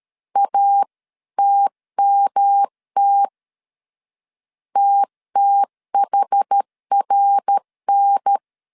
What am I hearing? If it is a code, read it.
Morse code, heard as a keyed tone, about 13 wpm, 800 Hz. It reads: ATMT TTHRN
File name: CW / DET / 1 3 1500C S A 12wpm 2905